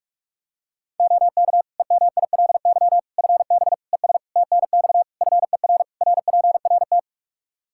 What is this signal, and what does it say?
Morse code, heard as a keyed tone, about 46 wpm, 700 Hz. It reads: OK EMILY FB ES TNX FER RPRT
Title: Practice QSO - Exchange 2 - Info - 45wpm